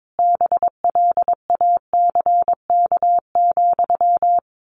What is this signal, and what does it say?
Morse code, heard as a keyed tone, about 22 wpm, 700 Hz. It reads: BLACK,
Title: A Sound Of Thunder 22wpm